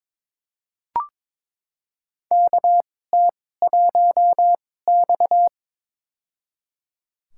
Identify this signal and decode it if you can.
Morse code, heard as a keyed tone, about 22 wpm, spaced as for 16 wpm, 700 Hz. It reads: KT1X
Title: USA Call Signs - 2x Character Spacing - 22wpm